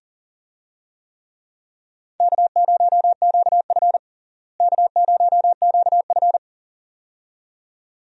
Morse code, heard as a keyed tone, about 40 wpm, 700 Hz. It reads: K0QF K0QF
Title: USA Call Signs - 4 Characters - Repeated 2x - 40wpm